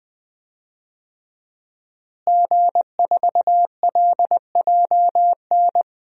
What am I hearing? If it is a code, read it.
Morse code, heard as a keyed tone, about 21 wpm, 700 Hz. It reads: G4LJN
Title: DXCC-100 Call Signs - 5 Characters - Repeated 2x - 20wpm